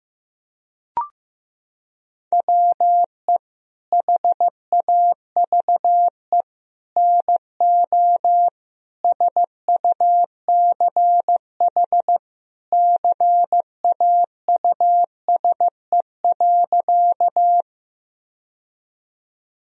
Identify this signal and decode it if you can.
Morse code, heard as a keyed tone, about 15 wpm, 700 Hz. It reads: WE HAVE NO SUCH CAUSE.
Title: Sentences from Top 1000 Words - Review Part 1 - 15wpm